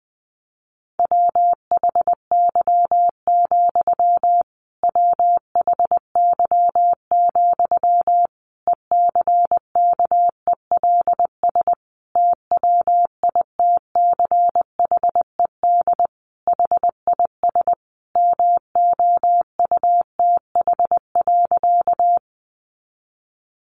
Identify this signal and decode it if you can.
Morse code, heard as a keyed tone, about 20 wpm, 700 Hz. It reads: WHY, WHY, ECKELS TWITCHED HIS MOUTH.
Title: A Sound Of Thunder 20wpm